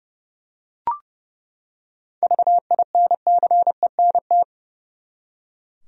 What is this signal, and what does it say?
Morse code, heard as a keyed tone, about 31 wpm, 700 Hz. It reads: VINCENT
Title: USA Common Names - 30wpm